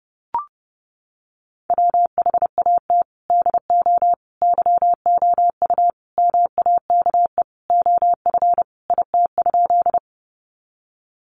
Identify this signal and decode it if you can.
Morse code, heard as a keyed tone, about 30 wpm, 700 Hz. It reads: WHAT DO YOU MAKE OF IT?
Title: Sentences from Top 100 Words 30wpm